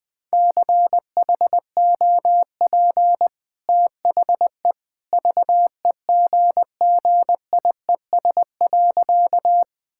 Morse code, heard as a keyed tone, about 20 wpm, 700 Hz. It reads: CHOP THE VEGGIES.
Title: Easy Sentences with Hints - 20wpm